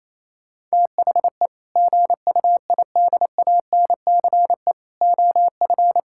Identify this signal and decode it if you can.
Morse code, heard as a keyed tone, about 28 wpm, 700 Hz. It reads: THE GUIDANCE OF
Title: Sentences from Top 1600 Words 28wpm